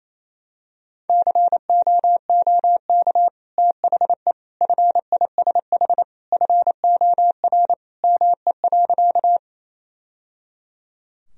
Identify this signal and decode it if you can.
Morse code, heard as a keyed tone, about 28 wpm, 700 Hz. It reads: COOK THE FISH FOR ME.
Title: Sentences from Top 1000 Words - Review Part 4 - 28wpm